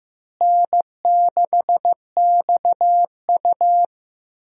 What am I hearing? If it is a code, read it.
Morse code, heard as a keyed tone, about 15 wpm, 700 Hz. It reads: N6XU
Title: USA Call Signs - 4 Characters - Repeated 3x - 15wpm